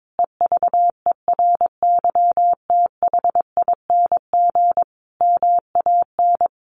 Morse code, heard as a keyed tone, about 22 wpm, 700 Hz. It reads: EVERYTHING MAN
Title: Sentences from Top 1100 Words 22wpm